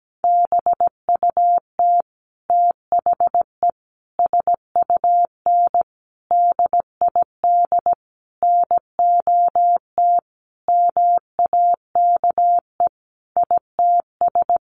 Morse code, heard as a keyed tone, about 17 wpm, 700 Hz. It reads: BUT THE SUN DID NOT MAKE ITS
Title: Sentences from Top 1100 Words 17wpm